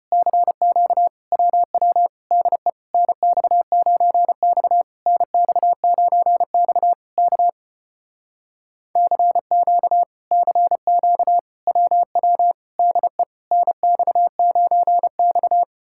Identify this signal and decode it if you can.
Morse code, heard as a keyed tone, about 33 wpm, 700 Hz. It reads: CQ WW DE NX9X NX9X K CQ CQ WW DE NX9X